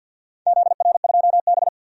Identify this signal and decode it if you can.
Morse code, heard as a keyed tone, about 52 wpm, 700 Hz. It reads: ZR2B
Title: DXCC-100 Call Signs - 4 Characters - 50wpm